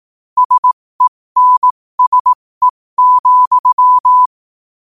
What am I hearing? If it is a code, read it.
Morse code, heard as a keyed tone, about 18 wpm, spaced as for 15 wpm, 1000 Hz. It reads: SENSE,
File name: qotdfiles / 15wpm